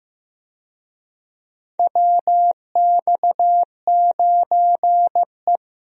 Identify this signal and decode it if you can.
Morse code, heard as a keyed tone, about 16 wpm, 700 Hz. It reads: WX9E
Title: USA Call Signs - 4 Characters - 15wpm